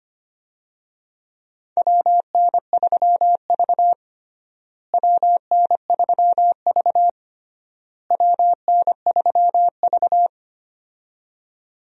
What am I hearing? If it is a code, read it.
Morse code, heard as a keyed tone, about 25 wpm, 700 Hz. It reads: WN3V WN3V WN3V
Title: USA Call Signs - 4 Characters - Repeated 3x - 25wpm